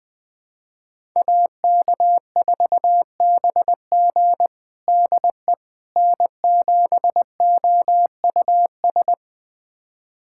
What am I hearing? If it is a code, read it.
Morse code, heard as a keyed tone, about 20 wpm, 700 Hz. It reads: AK4BG DE N7OUS
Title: USA Callsign DE Callsign - 5 Characters - 20wpm